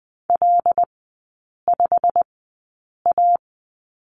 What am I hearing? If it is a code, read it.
Morse code, heard as a keyed tone, about 20 wpm, spaced as for 8 wpm, 700 Hz. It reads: L5A
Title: 093 Lesson 16e - Letter c - Three Characters 20wpm